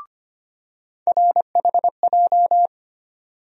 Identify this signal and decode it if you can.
Morse code, heard as a keyed tone, about 25 wpm, 700 Hz. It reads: RHJ
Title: USA Call Sign Postfixes - 3 Letters 25wpm